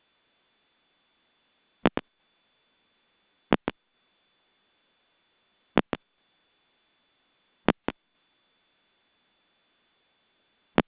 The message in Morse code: none